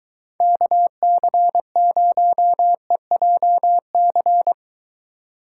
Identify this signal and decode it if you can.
Morse code, heard as a keyed tone, about 22 wpm, 700 Hz. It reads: KC0EJC